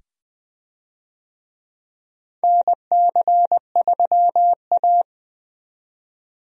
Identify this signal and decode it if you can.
Morse code, heard as a keyed tone, about 20 wpm, 700 Hz. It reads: NC3A
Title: USA Call Signs - 4 Characters - 20wpm